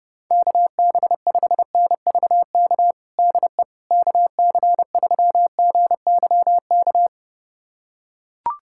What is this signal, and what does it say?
Morse code, heard as a keyed tone, about 30 wpm, 700 Hz. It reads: KB5NVK DE KC3GYK
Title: USA Callsign DE Callsign 30wpm